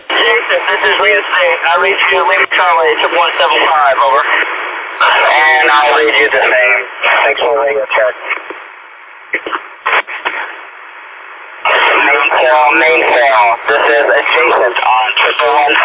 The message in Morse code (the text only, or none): none